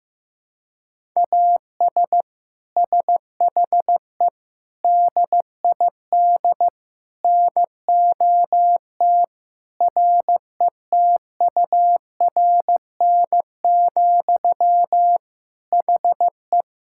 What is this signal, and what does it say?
Morse code, heard as a keyed tone, about 15 wpm, 700 Hz. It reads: AS SHE DID NOT RETURN, HE
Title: Sentences from Top 1000 Words - Review Part 5 - 15wpm